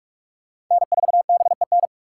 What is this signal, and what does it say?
Morse code, heard as a keyed tone, about 46 wpm, 690 Hz. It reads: N4BEN
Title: USA Call Signs - 5 Characters - 6x Character Spacing - 45wpm